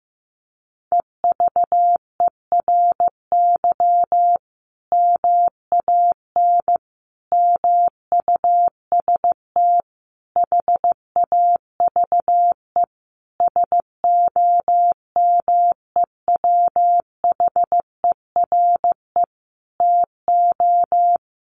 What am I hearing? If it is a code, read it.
Morse code, heard as a keyed tone, about 14 wpm, 700 Hz. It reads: EVERY MAN MUST HAVE SOMEWHERE TO
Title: Sentences from Top 1400 Words 15wpm